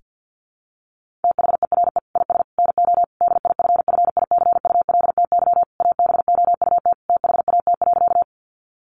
Morse code, heard as a keyed tone, about 50 wpm, 700 Hz. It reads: THERE IS NO DIFFICULTY ABOUT THAT.
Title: Sentences from Top 1500 Words 50wpm